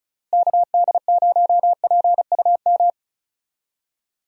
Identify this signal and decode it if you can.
Morse code, heard as a keyed tone, about 34 wpm, 700 Hz. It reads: KD0PUM
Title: USA Call Signs - 6 Characters - Repeated 3x - 35wpm